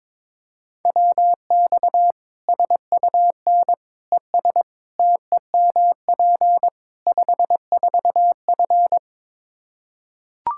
Morse code, heard as a keyed tone, about 21 wpm, 700 Hz. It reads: WX SUN ES TEMP 54F
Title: Practice QSO - Exchange 2 - Info - 22wpm